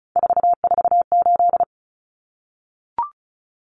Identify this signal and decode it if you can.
Morse code, heard as a keyed tone, about 34 wpm, 700 Hz. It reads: 448
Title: Numbers - 3-Digit 35wpm